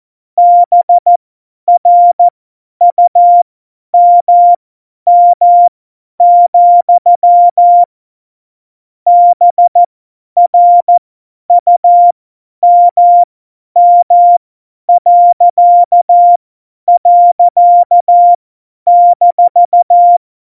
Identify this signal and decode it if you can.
Morse code, heard as a keyed tone, about 14 wpm, spaced as for 10 wpm, 700 Hz. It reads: BRUMM, BRUMM..-
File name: Morsekurs / Witze / Witz 23. Januar 2018, 50 F70